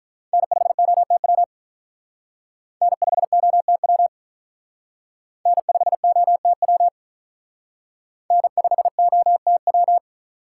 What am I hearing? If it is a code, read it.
Morse code, heard as a keyed tone, about 45 wpm, 700 Hz. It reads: N5OTW N5OTW N5OTW N5OTW